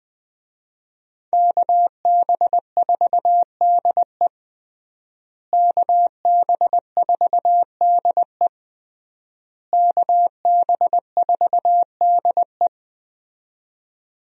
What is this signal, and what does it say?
Morse code, heard as a keyed tone, about 21 wpm, 700 Hz. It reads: KB4DE KB4DE KB4DE
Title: USA Call Signs - 5 Characters - Repeated 3x - 20wpm